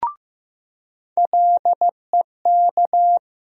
Morse code, heard as a keyed tone, about 15 wpm, 700 Hz. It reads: LEK
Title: USA Call Sign Postfixes - 3 Letters 15wpm